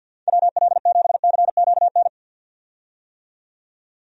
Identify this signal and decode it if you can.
Morse code, heard as a keyed tone, about 51 wpm, 700 Hz. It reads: WL7CQN
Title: USA Call Signs - 6 Characters - 50wpm